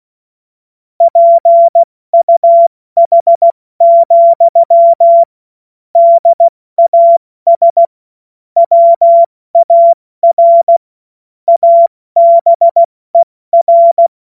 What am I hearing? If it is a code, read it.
Morse code, heard as a keyed tone, about 16 wpm, 680 Hz. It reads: PUH, DAS WAR ABER
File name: Morsekurs / Witze / Witz 18. Juli 2023 70 F80